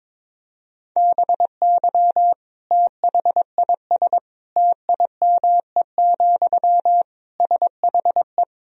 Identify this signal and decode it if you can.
Morse code, heard as a keyed tone, about 22 wpm, 700 Hz. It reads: BY THIS TIME, SHE
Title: Sentences from Top 1000 Words - Review Part 4 - 22wpm